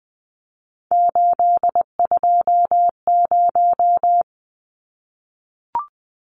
Morse code, heard as a keyed tone, about 20 wpm, 700 Hz. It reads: 820
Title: Numbers - 3-Digit 20wpm